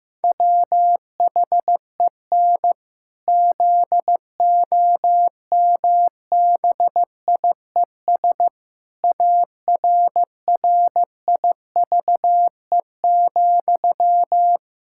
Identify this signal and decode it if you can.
Morse code, heard as a keyed tone, about 15 wpm, 700 Hz. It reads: WHEN ZOMBIES ARRIVE,
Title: Pangram Challenge v2 - 15wpm